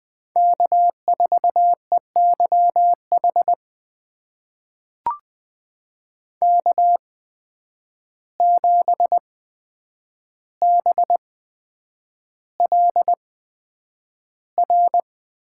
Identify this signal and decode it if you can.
Morse code, heard as a keyed tone, about 21 wpm, spaced as for 9 wpm, 700 Hz. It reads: K4EYH K7BLR
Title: USA Call Signs - 5 Characters - 8x Character Spacing - 20wpm